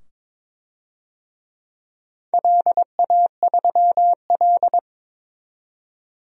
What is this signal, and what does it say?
Morse code, heard as a keyed tone, about 22 wpm, 700 Hz. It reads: LA3L
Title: DXCC-100 Call Signs - 4 Characters - 22wpm